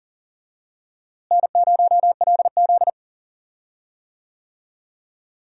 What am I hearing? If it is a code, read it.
Morse code, heard as a keyed tone, about 40 wpm, 700 Hz. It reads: N0LZ
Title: USA Call Signs - 4 Characters - Repeated 2x - 40wpm